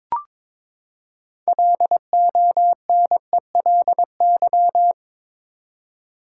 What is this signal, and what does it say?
Morse code, heard as a keyed tone, about 22 wpm, 700 Hz. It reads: LONELY
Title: Six Letter Words - 22wpm